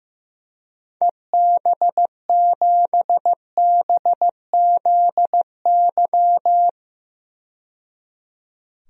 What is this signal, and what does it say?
Morse code, heard as a keyed tone, about 15 wpm, 700 Hz. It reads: EB7BZY
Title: DXCC-100 Call Signs - 6 Characters - 15wpm